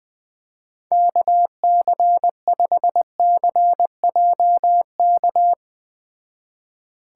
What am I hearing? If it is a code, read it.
Morse code, heard as a keyed tone, about 20 wpm, 700 Hz. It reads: KC5CJK
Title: Speed Racing - USA 2x3 Call Signs 17wpm